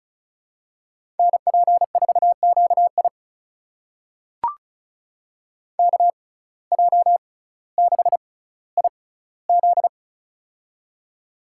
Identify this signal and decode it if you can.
Morse code, heard as a keyed tone, about 36 wpm, spaced as for 18 wpm, 700 Hz. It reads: NP4QI KJ6IZ
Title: USA Call Signs - 5 Characters - 6x Character Spacing - 35wpm